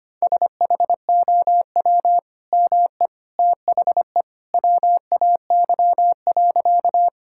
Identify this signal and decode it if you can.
Morse code, heard as a keyed tone, about 25 wpm, 700 Hz. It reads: SHOW ME THE WAY.
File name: practice / 25wpm / Sentences with 2-Word Phrases - 25wpm